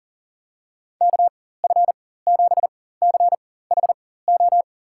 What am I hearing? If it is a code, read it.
Morse code, heard as a keyed tone, about 40 wpm, spaced as for 19 wpm, 700 Hz. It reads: KF7CHO
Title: USA Call Signs - 6 Characters - 4x Character Spacing - 40wpm